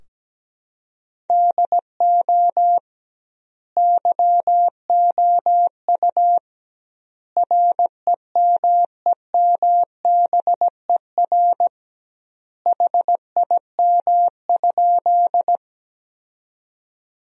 Morse code, heard as a keyed tone, about 17 wpm, 700 Hz. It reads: DO YOU REMEMBER HIM?
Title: 2x Word-Spacing - Sentences from Top 400 Words - 17wpm